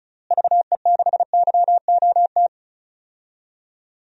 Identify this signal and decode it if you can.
Morse code, heard as a keyed tone, about 36 wpm, 700 Hz. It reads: VE6YOT
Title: DXCC-100 Call Signs - 6 Characters - Repeated 3x - 35wpm